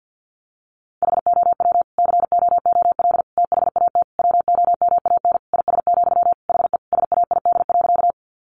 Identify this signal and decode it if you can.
Morse code, heard as a keyed tone, about 50 wpm, 700 Hz. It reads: HOW COOL THAT WOMAN IS, HE SAID.
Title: Sentences from Top 1200 Words 50wpm